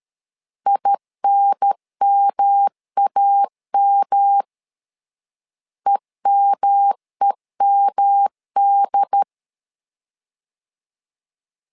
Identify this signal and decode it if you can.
Morse code, heard as a keyed tone, about 13 wpm, 800 Hz. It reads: INMAM EMEMD
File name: LEZIONI / 3 MWRO / 3 M 12wpm 1541